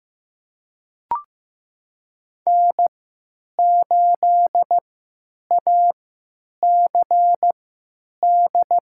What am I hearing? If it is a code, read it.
Morse code, heard as a keyed tone, about 15 wpm, spaced as for 9 wpm, 700 Hz. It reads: N8ACD
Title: USA Call Signs - 3x Character Spacing - 15wpm